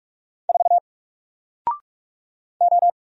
Morse code, heard as a keyed tone, about 46 wpm, 700 Hz. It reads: 4 O
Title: Single Letter-Number Alt 45wpm